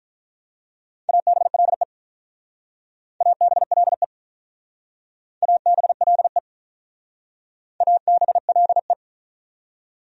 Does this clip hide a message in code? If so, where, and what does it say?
Yes: Morse code from 1.1 to 8.9 s, heard as a keyed tone, about 45 wpm, 700 Hz. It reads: ABLE ABLE ABLE ABLE